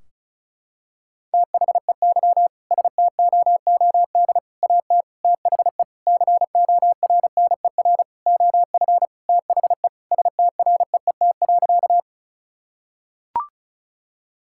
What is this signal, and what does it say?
Morse code, heard as a keyed tone, about 34 wpm, 700 Hz. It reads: THEY STOOD AT THE CORNER OF THE STREET.
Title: Sentences from Top 1600 Words 35wpm